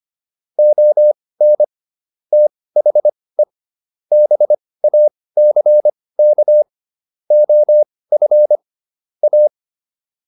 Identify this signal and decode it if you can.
Morse code, heard as a keyed tone, about 25 wpm, spaced as for 18 wpm, 600 Hz. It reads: ON THE BACK OF A